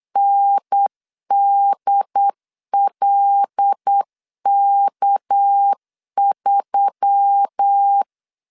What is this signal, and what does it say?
Morse code, heard as a keyed tone, about 8 wpm, 800 Hz. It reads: NDLK3
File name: CW / DET / 1 6 1500C S B 8wpm 5142